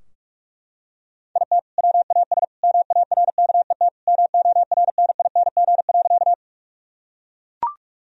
Morse code, heard as a keyed tone, about 43 wpm, 700 Hz. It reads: IT WAS MARKET MORNING.
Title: Sentences from Top 1000 Words - Review Part 3 - 45wpm